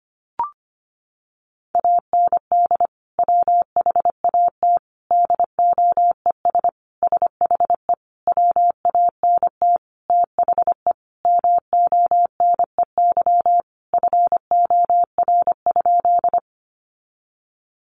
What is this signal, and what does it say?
Morse code, heard as a keyed tone, about 25 wpm, 700 Hz. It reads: AND WHAT DOES SHE WANT THE MONEY FOR?
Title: Sentences from Top 400 Words 25wpm